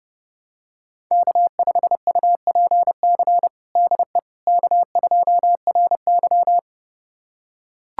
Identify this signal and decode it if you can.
Morse code, heard as a keyed tone, about 30 wpm, 700 Hz. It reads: K5UPC DE K2RY
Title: USA Callsign DE Callsign 30wpm